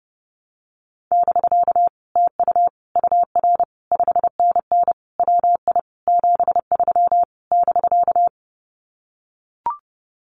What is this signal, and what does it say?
Morse code, heard as a keyed tone, about 32 wpm, 700 Hz. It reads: <BK> TU UR 5NN WI 73 <BK>
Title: Common QSO Phrases Encore 30wpm